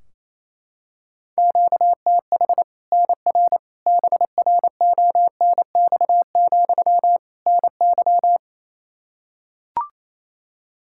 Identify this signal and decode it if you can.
Morse code, heard as a keyed tone, about 28 wpm, 700 Hz. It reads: QTH NR BRONX, NY
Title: Practice QSO - Exchange 1 - Beginning - 28wpm